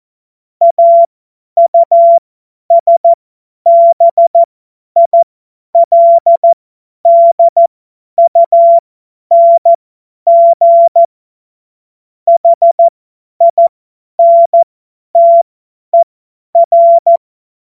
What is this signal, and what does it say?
Morse code, heard as a keyed tone, about 14 wpm, spaced as for 10 wpm, 680 Hz. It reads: AUSBILDUNG HINTER
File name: Morsekurs / Witze / Witz 29. Juni 2021, 50 F70